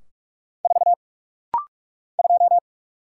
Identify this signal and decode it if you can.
Morse code, heard as a keyed tone, about 44 wpm, 700 Hz. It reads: 4 2
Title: Single Letter-Number Alt - Rapid-Fire 45wpm